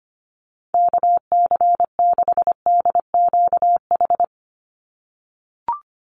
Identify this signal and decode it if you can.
Morse code, heard as a keyed tone, about 25 wpm, 700 Hz. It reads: KC6DQH
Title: USA Call Signs 25wpm